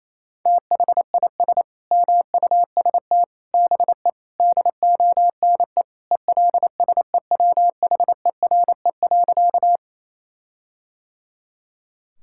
Morse code, heard as a keyed tone, about 28 wpm, 700 Hz. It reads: THIS MUST BE DONE ELSEWHERE.
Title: Sentences from Top 1600 Words 28wpm